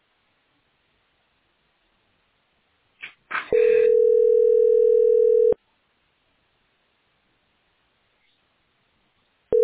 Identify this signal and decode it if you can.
Ringback tone (North America), 440 + 480 Hz.